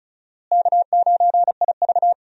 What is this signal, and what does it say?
Morse code, heard as a keyed tone, about 36 wpm, 700 Hz. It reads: K9IV
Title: USA Call Signs - 4 Characters - Repeated 3x - 35wpm